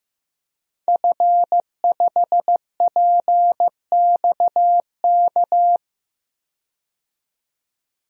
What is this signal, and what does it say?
Morse code, heard as a keyed tone, about 15 wpm, 700 Hz. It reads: F5PXK
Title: DXCC-100 Call Signs - 5 Characters - 15wpm